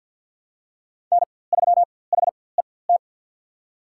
Morse code, heard as a keyed tone, about 53 wpm, spaced as for 24 wpm, 700 Hz. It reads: N3HET
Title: USA Call Signs - 5 Characters - 4x Character Spacing - 50wpm